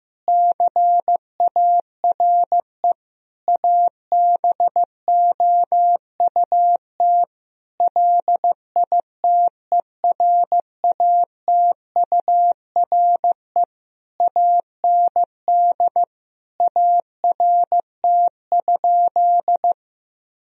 Morse code, heard as a keyed tone, about 15 wpm, 700 Hz. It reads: CARE ABOUT LITERATURE AND ART?
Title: Sentences from Top 1000 Words - Review Part 3 - 15wpm